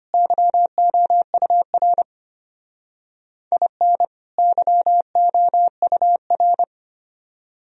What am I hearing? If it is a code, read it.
Morse code, heard as a keyed tone, about 32 wpm, 700 Hz. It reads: YOUR IN YOUR